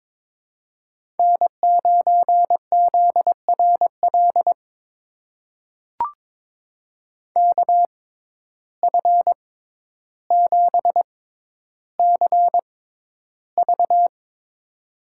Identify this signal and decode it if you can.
Morse code, heard as a keyed tone, about 22 wpm, spaced as for 11 wpm, 700 Hz. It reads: N9ZRL KF7CV
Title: USA Call Signs - 5 Characters - 6x Character Spacing - 22wpm